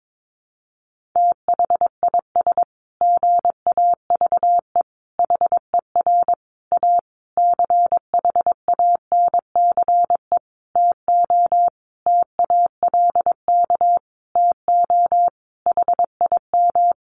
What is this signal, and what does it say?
Morse code, heard as a keyed tone, about 22 wpm, 700 Hz. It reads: THIS GAVE HER A CHANCE TO TALK TO HIM
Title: Sentences from Top 1400 Words 22wpm